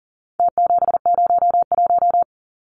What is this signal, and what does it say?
Morse code, heard as a keyed tone, about 40 wpm, 700 Hz. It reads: T701
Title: Numbers - Contest Style - 4-Digit 40wpm